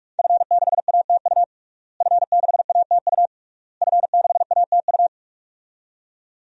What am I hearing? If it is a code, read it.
Morse code, heard as a keyed tone, about 47 wpm, 700 Hz. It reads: F6ATU F6ATU F6ATU
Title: DXCC-100 Call Signs - 5 Characters - Repeated 3x - 45wpm